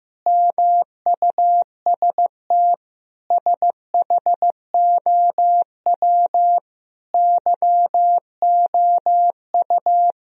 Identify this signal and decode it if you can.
Morse code, heard as a keyed tone, about 15 wpm, 700 Hz. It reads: MUST SHOW YOU
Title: Sentences from Top 1000 Words - Review Part 4 - 15wpm